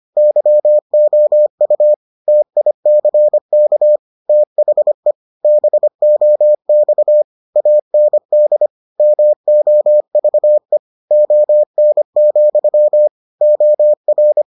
Morse code, heard as a keyed tone, about 25 wpm, 600 Hz. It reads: YOU TICK THE BOX AND MOVE ON, OR